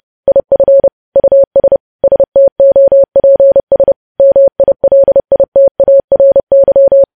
Morse code, heard as a keyed tone, about 30 wpm, 550 Hz. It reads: IF US STOPS MILITARY
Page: news headlines in morse code at 30 wpm